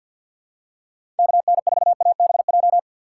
Morse code, heard as a keyed tone, about 53 wpm, 700 Hz. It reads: KN4ABJ
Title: USA Call Signs - 6 Characters - 3x Character Spacing - 50wpm